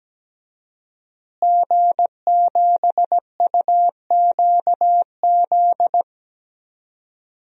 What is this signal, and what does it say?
Morse code, heard as a keyed tone, about 17 wpm, 700 Hz. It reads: G7UQZ
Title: DXCC-100 Call Signs - 5 Characters - Repeated 2x - 17wpm